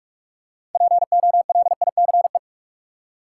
Morse code, heard as a keyed tone, about 45 wpm, 700 Hz. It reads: POLICE
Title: Six Letter Words - 45wpm